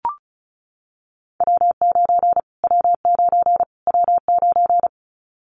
Morse code, heard as a keyed tone, about 34 wpm, 700 Hz. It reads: W9 W9 W9
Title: Callsign Prefixes for Top 25 DXCC - Repeated 3x - 35wpm